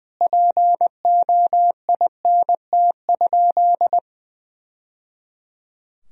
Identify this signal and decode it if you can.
Morse code, heard as a keyed tone, about 20 wpm, 700 Hz. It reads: POINT?
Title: Sentences from Top 1000 Words - Review Part 1 - 20wpm